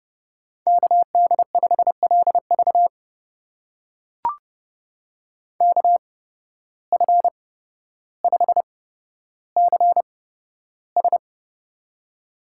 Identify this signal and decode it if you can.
Morse code, heard as a keyed tone, about 31 wpm, spaced as for 13 wpm, 700 Hz. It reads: KD5LV KF5CS
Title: USA Call Signs - 5 Characters - 8x Character Spacing - 30wpm